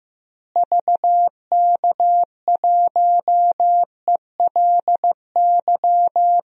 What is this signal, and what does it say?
Morse code, heard as a keyed tone, about 15 wpm, 700 Hz. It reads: VK1ELY
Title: DXCC-100 Call Signs - 6 Characters - Repeated 3x - 15wpm